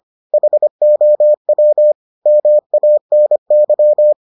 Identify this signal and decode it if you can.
Morse code, heard as a keyed tone, about 25 wpm, 600 Hz. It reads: HOW MANY